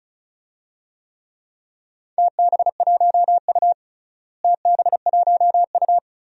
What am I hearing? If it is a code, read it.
Morse code, heard as a keyed tone, about 36 wpm, 700 Hz. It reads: TB1U TB1U
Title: DXCC-100 Call Signs - 4 Characters - Repeated 2x - 35wpm